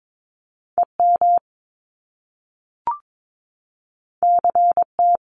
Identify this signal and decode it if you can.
Morse code, heard as a keyed tone, about 22 wpm, 700 Hz. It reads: EM CT
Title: Top Two Letter Combinations in Words 22wpm